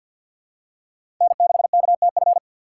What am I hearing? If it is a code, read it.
Morse code, heard as a keyed tone, about 52 wpm, 700 Hz. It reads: N6KTF
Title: USA Call Signs - 5 Characters - 6x Character Spacing - 50wpm